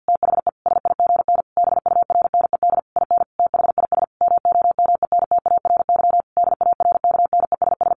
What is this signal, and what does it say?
Morse code, heard as a keyed tone, about 50 wpm, 700 Hz. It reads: THE SIGN BURNED IN THIS MOMENTARY DARKNESS
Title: A Sound Of Thunder 50wpm